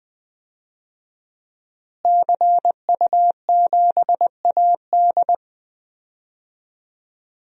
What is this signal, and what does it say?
Morse code, heard as a keyed tone, about 20 wpm, 700 Hz. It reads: CU7AD
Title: DXCC-100 Call Signs - 5 Characters - Repeated 3x - 20wpm